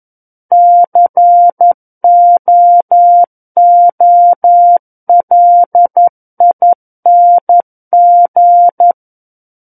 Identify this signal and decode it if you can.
Morse code, heard as a keyed tone, about 11 wpm, 700 Hz. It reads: COOLING
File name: 11 WPM